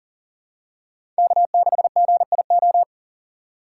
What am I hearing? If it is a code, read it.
Morse code, heard as a keyed tone, about 40 wpm, 690 Hz. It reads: K6GIO
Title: USA Call Signs - 3x Character Spacing - 40wpm